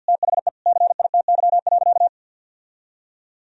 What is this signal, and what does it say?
Morse code, heard as a keyed tone, about 51 wpm, 700 Hz. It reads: THE CITY.
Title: Sentences from Top 1000 Words - Review Part 1 - 50wpm